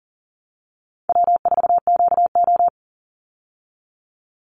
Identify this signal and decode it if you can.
Morse code, heard as a keyed tone, about 40 wpm, 700 Hz. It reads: W4QO
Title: Easy Call Signs Repeated 3x - 40wpm